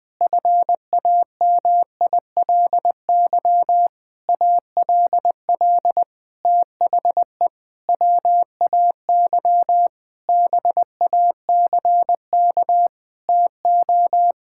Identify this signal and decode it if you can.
Morse code, heard as a keyed tone, about 20 wpm, 700 Hz. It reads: FAMILY ALL THE WAY BACK TO
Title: Sentences with 6-Word Phrases - 20wpm